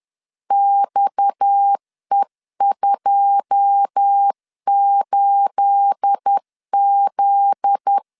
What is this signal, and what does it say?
Morse code, heard as a keyed tone, about 10 wpm, 800 Hz. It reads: XE28Z
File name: LEZIONI / 9 28 10wpm 2842